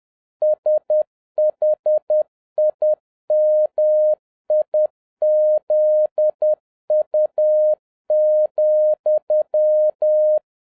Morse code, heard as a keyed tone, about 10 wpm, 610 Hz. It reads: SHIMIZU,